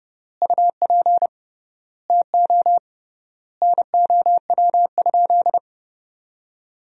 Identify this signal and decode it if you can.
Morse code, heard as a keyed tone, about 30 wpm, 700 Hz. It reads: UP TO NOW?
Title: 3x Word-Spacing - Sentences from Top 100 Words - 30wpm